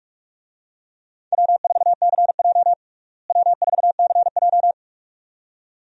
Morse code, heard as a keyed tone, about 41 wpm, 700 Hz. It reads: W4CJ W4CJ
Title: USA Call Signs - 4 Characters - Repeated 2x - 45wpm